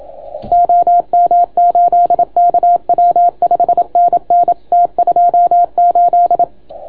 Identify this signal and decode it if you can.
Morse code, heard as a keyed tone, about 27 wpm, 690 Hz. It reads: OM8KW 5NN T28